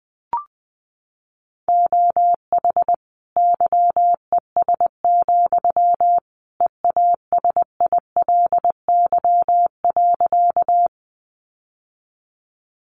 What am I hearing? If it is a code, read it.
Morse code, heard as a keyed tone, about 20 wpm, 700 Hz. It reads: OH YES, EASILY.
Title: Sentences from Top 900 Words 20wpm